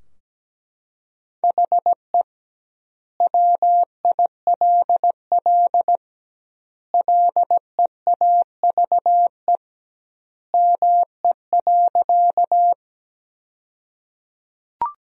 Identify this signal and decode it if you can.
Morse code, heard as a keyed tone, about 17 wpm, 700 Hz. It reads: HE WILL LEAVE ME.
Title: 2x Word-Spacing - Sentences from Top 400 Words - 17wpm